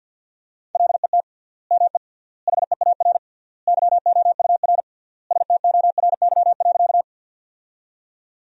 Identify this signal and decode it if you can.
Morse code, heard as a keyed tone, about 50 wpm, 700 Hz. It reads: LET ME HEAR YOUR STORY.
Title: 3x Word-Spacing - Sentences from Top 300 Words - 50wpm